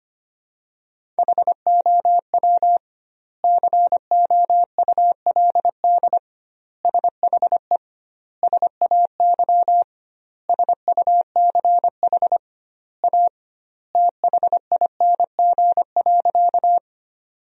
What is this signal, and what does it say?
Morse code, heard as a keyed tone, about 25 wpm, 700 Hz. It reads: HOW COULD SHE SAY SUCH A THING.
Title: 2x Word-Spacing - Sentences from Top 200 Words - 25wpm